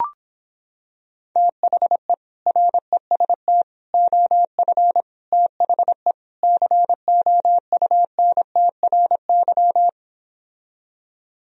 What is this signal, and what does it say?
Morse code, heard as a keyed tone, about 26 wpm, 700 Hz. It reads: THE REST OF THE COUNTRY